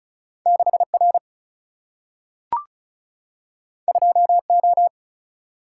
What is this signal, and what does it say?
Morse code, heard as a keyed tone, about 36 wpm, 700 Hz. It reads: 6 R 2 O
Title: Single-Letter Single-Number 35wpm